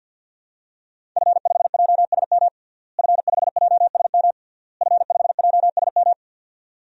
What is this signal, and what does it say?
Morse code, heard as a keyed tone, about 51 wpm, 700 Hz. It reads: F5JSM F5JSM F5JSM
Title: DXCC-100 Call Signs - 5 Characters - Repeated 3x - 50wpm